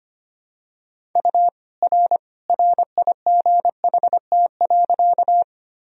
Morse code, heard as a keyed tone, about 25 wpm, 700 Hz. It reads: U R RIGHT.
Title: Sentences from Top 200 Words and Abbreviations 25wpm